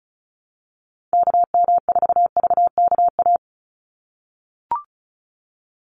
Morse code, heard as a keyed tone, about 34 wpm, 700 Hz. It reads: KM4VKA
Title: USA Call Signs 35wpm